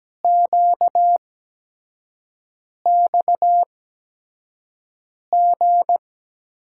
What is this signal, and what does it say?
Morse code, heard as a keyed tone, about 17 wpm, spaced as for 5 wpm, 700 Hz. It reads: QXG
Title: USA Call Signs - 5 Characters - 8x Character Spacing - 17wpm